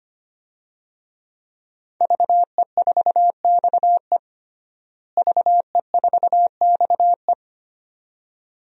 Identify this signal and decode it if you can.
Morse code, heard as a keyed tone, about 25 wpm, 700 Hz. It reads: VE4XE VE4XE
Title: DXCC-100 Call Signs - 5 Characters - Repeated 2x - 25wpm